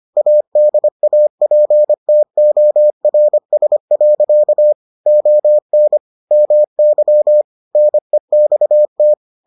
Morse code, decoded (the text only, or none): ADAPTORS. ON MY NEXT